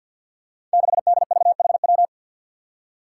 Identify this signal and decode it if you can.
Morse code, heard as a keyed tone, about 51 wpm, 700 Hz. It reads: 6DVHW
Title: Five Challenging Letter-Number Combinations 50wpm